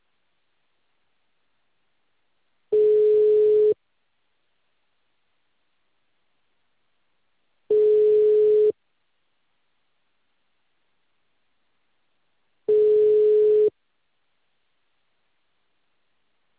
Ringback tone (Europe, 425 Hz).